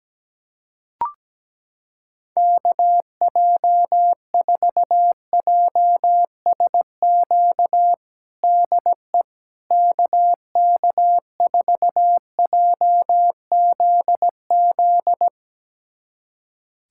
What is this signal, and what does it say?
Morse code, heard as a keyed tone, about 17 wpm, 700 Hz. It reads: KJ4JSQ DE KK4JZZ
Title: USA Callsign DE Callsign 17wpm